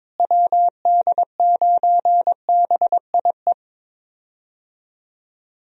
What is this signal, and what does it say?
Morse code, heard as a keyed tone, about 22 wpm, 700 Hz. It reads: WD9BIE
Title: USA Call Signs - 6 Characters - Repeated 3x - 22wpm